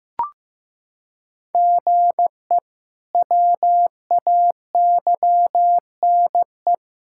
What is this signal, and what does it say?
Morse code, heard as a keyed tone, about 16 wpm, 700 Hz. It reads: GE WAYNE
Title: K1USN SST QSO Practice - 15wpm